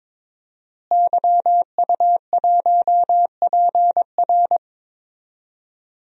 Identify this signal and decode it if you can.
Morse code, heard as a keyed tone, about 21 wpm, 700 Hz. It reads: YU1PR
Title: DXCC-100 Call Signs - 5 Characters - Repeated 2x - 22wpm